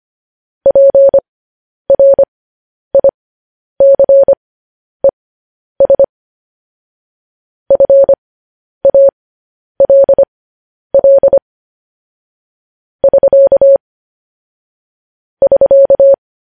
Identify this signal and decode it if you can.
Morse code, heard as a keyed tone, about 25 wpm, spaced as for 10 wpm, 550 Hz. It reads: PRICES FALL <SK> <SK>
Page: news headlines in morse code at 10 wpm